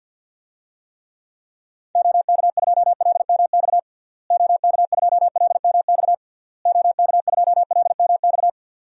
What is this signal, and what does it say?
Morse code, heard as a keyed tone, about 51 wpm, 700 Hz. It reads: OK2LMX OK2LMX OK2LMX
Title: DXCC-100 Call Signs - 6 Characters - Repeated 3x - 50wpm